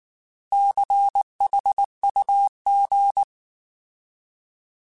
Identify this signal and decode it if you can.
Morse code, heard as a keyed tone, about 19 wpm, 800 Hz. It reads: CHUG